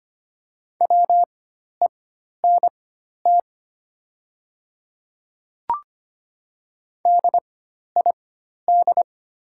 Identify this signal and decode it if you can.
Morse code, heard as a keyed tone, about 25 wpm, spaced as for 12 wpm, 700 Hz. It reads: WENT DID
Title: 4x Character Spacing - Top 300 Words - 25wpm